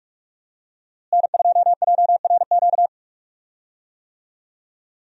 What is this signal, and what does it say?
Morse code, heard as a keyed tone, about 43 wpm, 700 Hz. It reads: N2JRQ
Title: USA Call Signs - 5 Characters - Repeated 3x - 45wpm